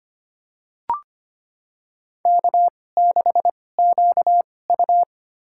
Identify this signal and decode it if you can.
Morse code, heard as a keyed tone, about 25 wpm, spaced as for 18 wpm, 700 Hz. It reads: K6QU
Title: USA Call Signs - 4 Characters - 2x Character Spacing - 25wpm